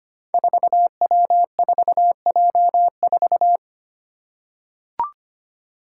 Morse code, heard as a keyed tone, about 25 wpm, 700 Hz. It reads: 4W4J4
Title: Five Challenging Letter-Number Combinations 25wpm